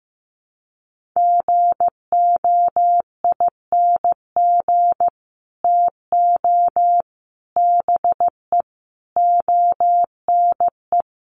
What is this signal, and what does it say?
Morse code, heard as a keyed tone, about 15 wpm, 700 Hz. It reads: GOING TO BE ONE
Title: Sets of 5 Words Encore2 15wpm